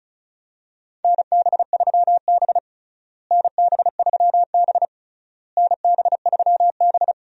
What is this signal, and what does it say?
Morse code, heard as a keyed tone, about 36 wpm, 700 Hz. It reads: NB3B NB3B NB3B
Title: USA Call Signs - 4 Characters - Repeated 3x - 35wpm